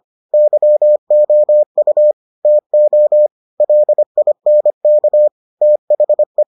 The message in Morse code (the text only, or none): YOU TO LINK THE